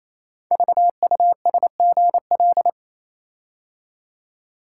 Morse code, heard as a keyed tone, about 29 wpm, 700 Hz. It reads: VUSGL
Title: Code Groups - 5 Letters - 28wpm